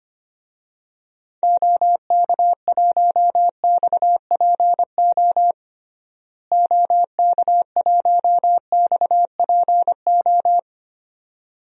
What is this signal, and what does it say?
Morse code, heard as a keyed tone, about 25 wpm, 700 Hz. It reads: OK1XPO OK1XPO
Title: DXCC-100 Call Signs - 6 Characters - Repeated 2x - 25wpm